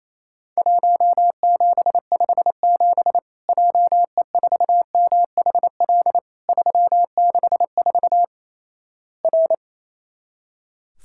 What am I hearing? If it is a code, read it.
Morse code, heard as a keyed tone, about 28 wpm, 700 Hz. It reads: 1757 JE4MHL 364 R
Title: Worked All Europe DX Contest - QTC Exchange - 28wpm